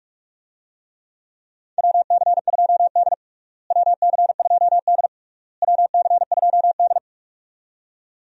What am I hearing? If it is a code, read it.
Morse code, heard as a keyed tone, about 47 wpm, 700 Hz. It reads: WC2D WC2D WC2D
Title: USA Call Signs - 4 Characters - Repeated 3x - 45wpm